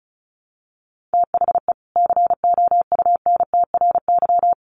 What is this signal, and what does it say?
Morse code, heard as a keyed tone, about 34 wpm, 700 Hz. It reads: THE COUNTRY
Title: Sets of 2 Words 35wpm